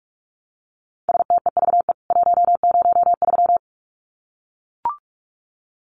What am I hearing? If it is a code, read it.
Morse code, heard as a keyed tone, about 44 wpm, 700 Hz. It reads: STEVE 103
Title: CWT Exchange 45wpm